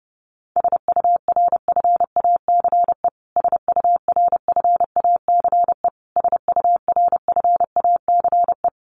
Morse code, heard as a keyed tone, about 28 wpm, 700 Hz. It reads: SURFACE SURFACE SURFACE
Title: Top 500 Words Repeated 30wpm